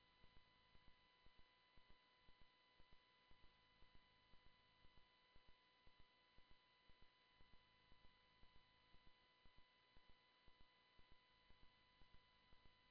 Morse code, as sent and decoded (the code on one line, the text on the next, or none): none